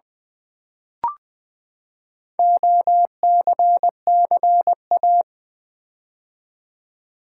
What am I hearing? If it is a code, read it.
Morse code, heard as a keyed tone, about 20 wpm, 700 Hz. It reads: OCCA
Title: Build A Word - 4 Letters At A Time - 8-Letter Words - 20wpm